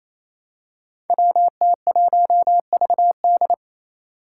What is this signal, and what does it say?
Morse code, heard as a keyed tone, about 29 wpm, 700 Hz. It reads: WT1VD
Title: Five Challenging Letter-Number Combinations 28wpm